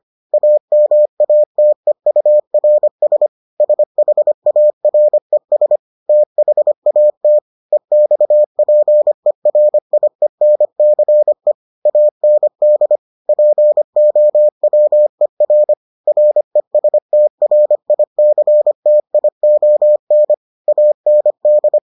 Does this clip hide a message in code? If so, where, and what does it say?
Yes: Morse code from 0.3 to 21.8 s, heard as a keyed tone, about 25 wpm, 600 Hz. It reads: AMATEURS SHARES THAT EXPERIENCE AND POWER RESTRICTION AND